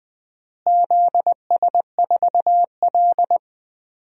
Morse code, heard as a keyed tone, about 19 wpm, 700 Hz. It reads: ZS4L
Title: DXCC-100 Call Signs - 4 Characters - Repeated 2x - 20wpm